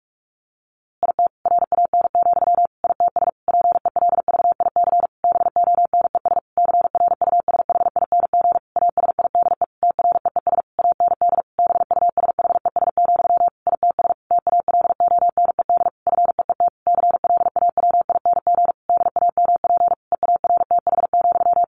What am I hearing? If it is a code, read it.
Morse code, heard as a keyed tone, about 44 wpm, 700 Hz. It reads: IT RAN, ITS PELVIC BONES CRUSHING ASIDE TREES AND BUSHES, ITS TALONED FEET CLAWING DAMP EARTH,
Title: A Sound Of Thunder 45wpm